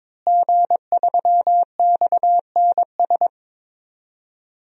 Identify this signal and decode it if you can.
Morse code, heard as a keyed tone, about 22 wpm, 700 Hz. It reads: G3XNS
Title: DXCC-100 Call Signs - 5 Characters - Repeated 3x - 22wpm